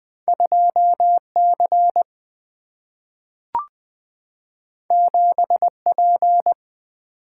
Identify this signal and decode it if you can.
Morse code, heard as a keyed tone, about 20 wpm, 700 Hz. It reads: 2 C 7 P
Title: Single-Letter Single-Number 20wpm